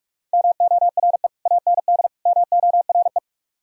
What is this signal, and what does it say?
Morse code, heard as a keyed tone, about 44 wpm, 700 Hz. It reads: MORE AND MORE